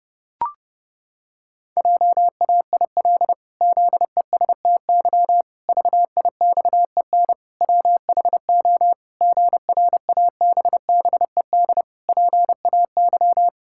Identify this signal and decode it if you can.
Morse code, heard as a keyed tone, about 30 wpm, 700 Hz. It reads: JAIL ZESTY VIXEN WHO GRABBED PAY
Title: Pangram Challenge v2 - 30wpm